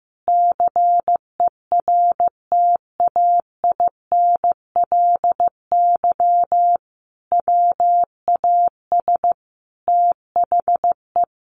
Morse code, heard as a keyed tone, about 15 wpm, 700 Hz. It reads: CERTAINLY WAS THE
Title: Sentences from Top 1100 Words 15wpm